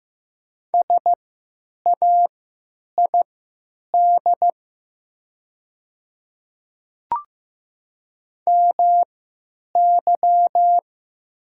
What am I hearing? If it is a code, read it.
Morse code, heard as a keyed tone, about 15 wpm, spaced as for 9 wpm, 700 Hz. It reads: SAID MY
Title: 3x Character Spacing - Top 100 Words - 15wpm